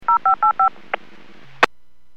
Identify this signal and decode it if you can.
DTMF, heard as telephone touch tones, tones of about 110 ms. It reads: 0585
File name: WPS / 00117 14-49 WPS 451.25 WPS